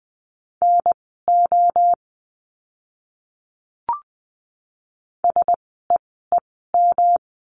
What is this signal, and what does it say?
Morse code, heard as a keyed tone, about 20 wpm, spaced as for 14 wpm, 700 Hz. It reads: NO SEEM
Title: 115 Lesson 19f - Letter w - Words 20wpm